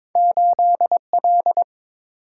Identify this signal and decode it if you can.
Morse code, heard as a keyed tone, about 23 wpm, 700 Hz. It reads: 8L
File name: practice / 22wpm / Two Letter-Number Combinations - 22wpm